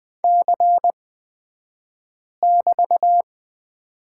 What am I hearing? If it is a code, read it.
Morse code, heard as a keyed tone, about 20 wpm, spaced as for 5 wpm, 700 Hz. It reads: C<BT>
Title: 223 Lesson 36d - BT - Two Characters 20wpm